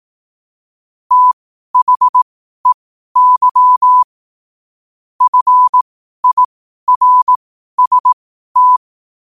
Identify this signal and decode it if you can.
Morse code, heard as a keyed tone, about 18 wpm, spaced as for 12 wpm, 1000 Hz. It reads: THEY FIRST
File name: qotdfiles / 12wpm